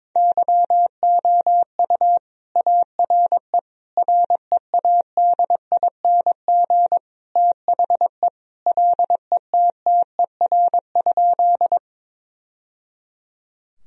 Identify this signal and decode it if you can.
Morse code, heard as a keyed tone, about 21 wpm, 700 Hz. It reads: YOU ARE READING THE LETTER?
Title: Sentences from Top 1000 Words - Review Part 3 - 22wpm